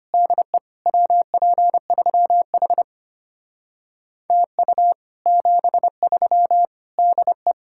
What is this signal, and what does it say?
Morse code, heard as a keyed tone, about 31 wpm, 700 Hz. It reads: DE WP3H TU 73 DE
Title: Speed-Racing Common QSO Phrases - 25wpm